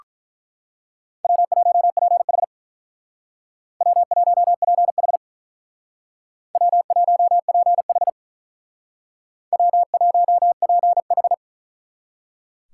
Morse code, heard as a keyed tone, about 42 wpm, 700 Hz. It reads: W1PH W1PH W1PH W1PH
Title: Speed Racing - USA 1x2 Call Signs 35wpm